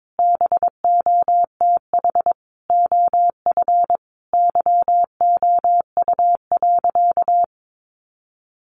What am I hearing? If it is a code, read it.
Morse code, heard as a keyed tone, about 22 wpm, 700 Hz. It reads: BOTH OF YOU.
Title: Sentences from Top 1200 Words 22wpm